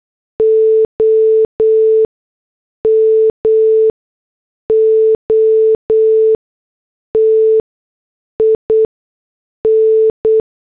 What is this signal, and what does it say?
Morse code, heard as a keyed tone, about 8 wpm, 440 Hz. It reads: O M O T I N
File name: morse / headlines